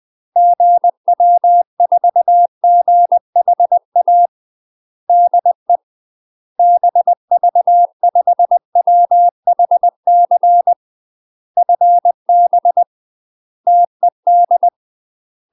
Morse code, heard as a keyed tone, about 20 wpm, 700 Hz. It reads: GW4GHA DE BV5WHC FB TED
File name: UK MASTER QSOs TRAINING / 20WPM / UK MASTER QSOs 0062